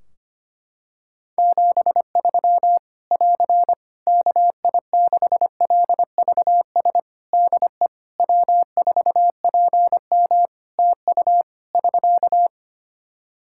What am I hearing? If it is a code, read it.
Morse code, heard as a keyed tone, about 25 wpm, 700 Hz. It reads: 73 <AR> KI6LVS DE W4PM TU <SK>
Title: Practice QSO - Exchange 4 - Ending - 25wpm